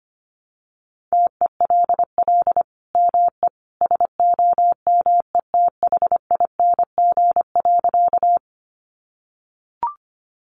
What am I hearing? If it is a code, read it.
Morse code, heard as a keyed tone, about 25 wpm, 700 Hz. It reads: TELL ME SOMETHING.
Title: Sentences from Top 600 Words 25wpm